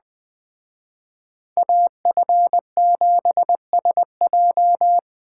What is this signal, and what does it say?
Morse code, heard as a keyed tone, about 20 wpm, 700 Hz. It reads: AF7SJ
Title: Speed Racing - USA 2x2 Call Signs 17wpm